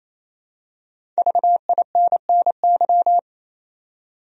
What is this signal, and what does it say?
Morse code, heard as a keyed tone, about 28 wpm, 700 Hz. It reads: VINNY
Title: USA Common Names - Max 5 Letters - 28wpm